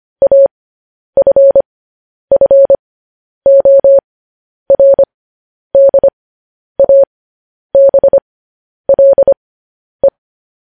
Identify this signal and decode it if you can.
Morse code, heard as a keyed tone, about 25 wpm, spaced as for 10 wpm, 550 Hz. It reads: AFFORDABLE
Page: news-headlines-in-morse-code-at-10-wpm